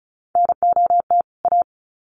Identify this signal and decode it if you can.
Morse code, heard as a keyed tone, about 34 wpm, 700 Hz. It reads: NOT A
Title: Sets of 2 Words 35wpm